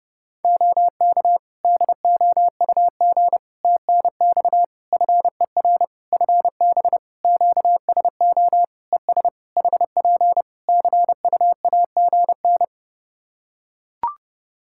Morse code, heard as a keyed tone, about 30 wpm, 700 Hz. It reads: OK DOUG TNX FER FB QSO ES HP CUAGN
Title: Practice QSO - Exchange 4 - Ending - 30wpm